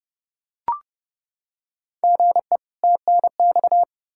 Morse code, heard as a keyed tone, about 32 wpm, 700 Hz. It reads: GE TNX
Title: Practice QSO - Exchange 1 - Beginning - 30wpm